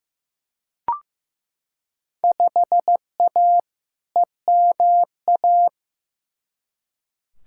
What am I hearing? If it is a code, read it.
Morse code, heard as a keyed tone, about 15 wpm, 700 Hz. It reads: 5A EMA
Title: Field Day Exchanges - 15wpm